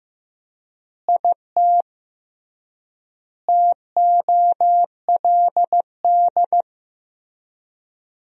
Morse code, heard as a keyed tone, about 16 wpm, 700 Hz. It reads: IT TOLD
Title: 3x Word-Spacing - Sentences from Top 400 Words - 15wpm